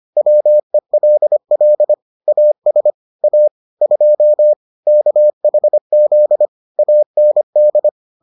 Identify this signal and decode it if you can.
Morse code, heard as a keyed tone, about 25 wpm, 600 Hz. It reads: WELL AS A 2 KHZ AND